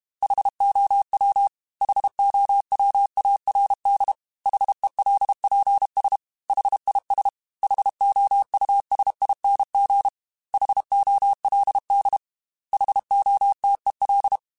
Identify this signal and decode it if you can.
Morse code, heard as a keyed tone, about 32 wpm, 800 Hz. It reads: HOW HOWARD HELPS HIS HOUSING HOLD HOTEL
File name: H Runs / H Runs mp3s / H Runs @30